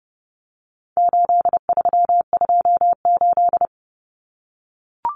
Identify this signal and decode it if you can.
Morse code, heard as a keyed tone, about 30 wpm, 700 Hz. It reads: 8328
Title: Numbers - 4-Digit 30wpm